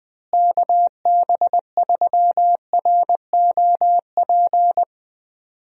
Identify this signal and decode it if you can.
Morse code, heard as a keyed tone, about 20 wpm, 700 Hz. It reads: KB3ROP
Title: USA Call Signs - 6 Characters - Repeated 3x - 20wpm